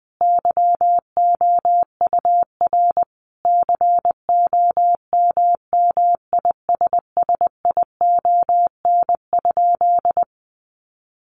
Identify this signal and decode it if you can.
Morse code, heard as a keyed tone, about 20 wpm, 700 Hz. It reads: YOUR COMMISSION?
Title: Sentences from Top 1400 Words 20wpm